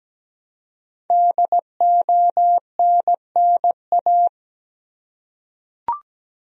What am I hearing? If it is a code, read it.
Morse code, heard as a keyed tone, about 17 wpm, 700 Hz. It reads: DONNA
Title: USA Common Names - 17wpm